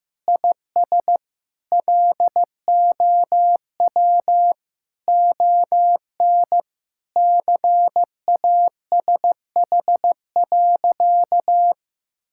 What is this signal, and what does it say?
Morse code, heard as a keyed tone, about 15 wpm, 700 Hz. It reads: IS LOW ON CASH.
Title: Sentences from Top 1000 Words - Review Part 5 - 15wpm